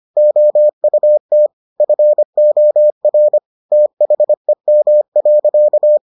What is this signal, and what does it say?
Morse code, heard as a keyed tone, about 25 wpm, 600 Hz. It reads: OUT FOR THEM.